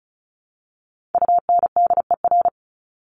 Morse code, heard as a keyed tone, about 34 wpm, 700 Hz. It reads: UNDER
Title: Top 200 Words Repeated 35wpm